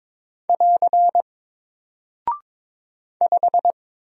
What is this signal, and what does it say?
Morse code, heard as a keyed tone, about 22 wpm, 700 Hz. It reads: <AR> 5
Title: Single Character (Letter:Number:Punctuation:Prosign) Alt 22wpm